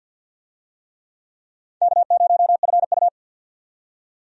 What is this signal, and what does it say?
Morse code, heard as a keyed tone, about 52 wpm, 700 Hz. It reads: K0FU
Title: USA Call Signs - 4 Characters - 50wpm